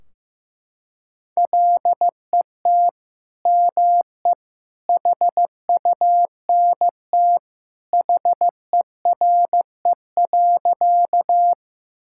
Morse code, heard as a keyed tone, about 15 wpm, 700 Hz. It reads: LET ME HUNT HERE.
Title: Sentences from Top 1600 Words 15wpm